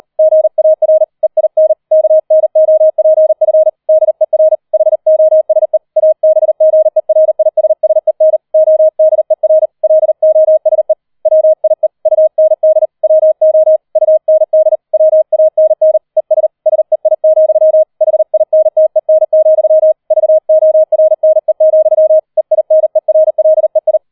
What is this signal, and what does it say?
Morse code, heard as a keyed tone, about 38 wpm, 630 Hz. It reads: GAR EIN KNOPF DER HOSE ABGERISSEN ODER LOSE WIE UND WO UND WANN ES SEI, HINTEN, VORNE, EINERLEI
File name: 003 Dritter Streich 40wpm